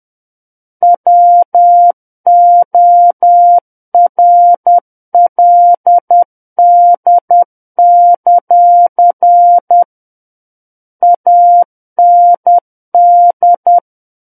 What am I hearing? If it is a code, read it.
Morse code, heard as a keyed tone, about 10 wpm, 700 Hz. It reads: WORLD; AND
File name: Book One / 01-Chapter 1 - 10 WPM